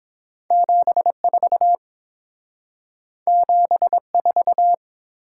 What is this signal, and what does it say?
Morse code, heard as a keyed tone, about 30 wpm, 700 Hz. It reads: 74 74